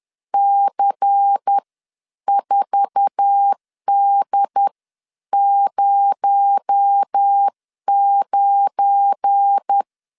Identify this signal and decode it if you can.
Morse code, heard as a keyed tone, about 10 wpm, spaced as for 9 wpm, 800 Hz. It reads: C4D09
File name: LEZIONI / 8 YJ45 / 8 YJ4 10wpm 2633